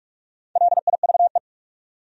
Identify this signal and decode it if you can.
Morse code, heard as a keyed tone, about 58 wpm, 700 Hz. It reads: LIVE